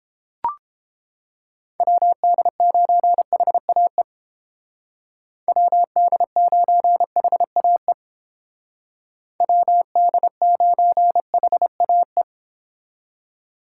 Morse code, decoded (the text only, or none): WD9HAE WD9HAE WD9HAE